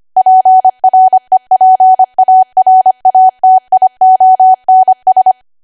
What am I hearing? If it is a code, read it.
Morse code, heard as a keyed tone, about 25 wpm, 750 Hz. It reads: PREPARATIONS
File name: Archive / 25 WPM / 25WPM